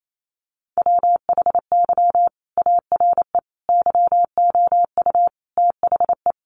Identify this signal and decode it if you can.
Morse code, heard as a keyed tone, about 28 wpm, 700 Hz. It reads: WHY ARE YOU THE
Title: Sentences from Top 1200 Words 28wpm